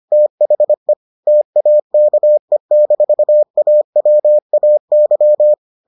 Morse code, heard as a keyed tone, about 25 wpm, 600 Hz. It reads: THE TAKE-AWAY